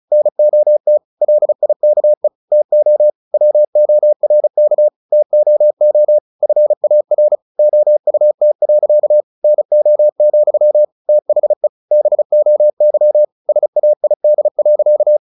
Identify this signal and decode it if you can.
Morse code, heard as a keyed tone, about 34 wpm, 600 Hz. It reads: NOT LIKE TO WORK TOO FAR OUT. NO, THE BOY SAID.
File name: the old man and the sea / the old man and the sea 1 35wpm